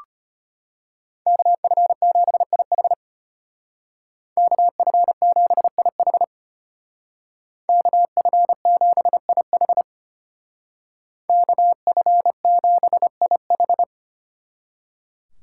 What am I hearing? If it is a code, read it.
Morse code, heard as a keyed tone, about 30 wpm, 700 Hz. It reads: KF7IH KF7IH KF7IH KF7IH
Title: Speed Racing - USA 2x2 Call Signs 25wpm